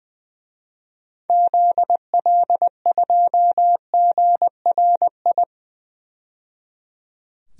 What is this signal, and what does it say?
Morse code, heard as a keyed tone, about 20 wpm, 700 Hz. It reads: ZL2GRI
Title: DXCC-100 Call Signs - 6 Characters - 20wpm